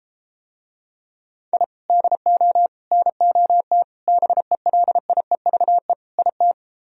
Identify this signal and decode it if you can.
Morse code, heard as a keyed tone, about 32 wpm, 700 Hz. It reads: I DO NOT BELIEVE IT